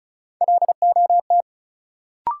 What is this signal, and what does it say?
Morse code, heard as a keyed tone, about 34 wpm, 700 Hz. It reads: LOT
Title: Sets of 2 Words - New Word Drill - 35wpm